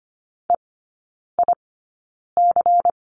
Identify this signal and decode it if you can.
Morse code, heard as a keyed tone, about 25 wpm, spaced as for 9 wpm, 700 Hz. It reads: EIC
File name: course / 25:10wpm / 093 Lesson 16e - Letter c - Three Characters 25wpm